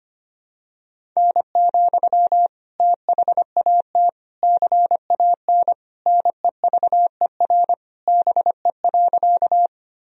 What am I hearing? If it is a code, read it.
Morse code, heard as a keyed tone, about 25 wpm, 700 Hz. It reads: N, THAT CAN NEVER BE.
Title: Sentences from Top 300 Words and Abbreviations 25wpm